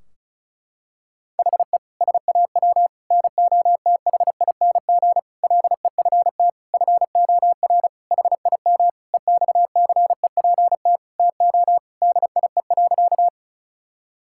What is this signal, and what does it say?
Morse code, heard as a keyed tone, about 35 wpm, 700 Hz. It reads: HE SAW NOTHING LEFT FOR HIM EXCEPT TO DIE.
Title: Sentences from Top 1600 Words 35wpm